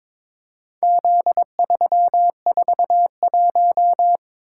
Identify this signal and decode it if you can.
Morse code, heard as a keyed tone, about 22 wpm, 700 Hz. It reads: Z341
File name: practice / 22wpm / Four Letter-Number Combinations - 22wpm